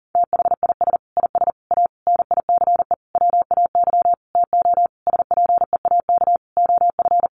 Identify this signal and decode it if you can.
Morse code, heard as a keyed tone, about 40 wpm, 700 Hz. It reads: THIS IS A NICE WAY TO SPEAK OF
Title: Sentences from Top 1100 Words 40wpm